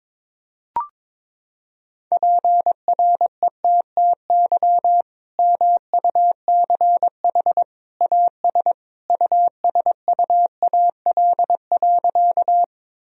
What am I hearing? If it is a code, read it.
Morse code, heard as a keyed tone, about 22 wpm, 700 Hz. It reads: PRETTY MUCH AS USUAL.
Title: Sentences from Top 1000 Words - Review Part 2 - 22wpm